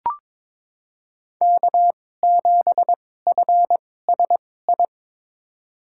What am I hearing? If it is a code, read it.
Morse code, heard as a keyed tone, about 22 wpm, spaced as for 16 wpm, 700 Hz. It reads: K7FSI
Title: USA Call Signs - 5 Characters - 2x Character Spacing - 22wpm